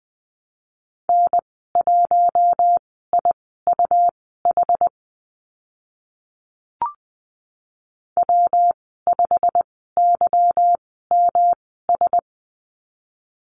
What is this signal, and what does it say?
Morse code, heard as a keyed tone, about 20 wpm, spaced as for 15 wpm, 700 Hz. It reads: N1IUH W5YMS
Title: 147 Lesson 24g - Letter y - Call Signs 20wpm